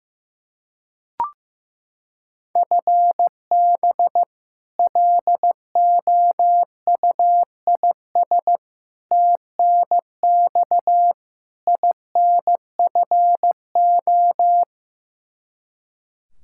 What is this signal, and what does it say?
Morse code, heard as a keyed tone, about 15 wpm, 700 Hz. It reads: FB LOUIS TNX INFO
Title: Practice QSO - Exchange 3 - More Info - 15wpm